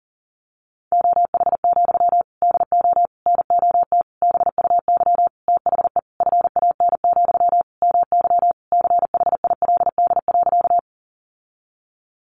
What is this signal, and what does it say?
Morse code, heard as a keyed tone, about 37 wpm, 700 Hz. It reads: OH, DO NOT BUY THE FAN, MY CHILD.
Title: Sentences from Top 1300 Words 40wpm